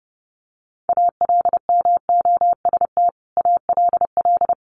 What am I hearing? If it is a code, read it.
Morse code, heard as a keyed tone, about 30 wpm, 700 Hz. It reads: ALMOST ALL
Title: Sentences from Top 1500 Words 30wpm